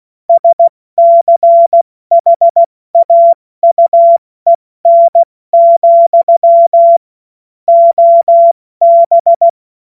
Morse code, heard as a keyed tone, about 16 wpm, 680 Hz. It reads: SCHAUEN, OB
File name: Witz-16-September-2025-70-F80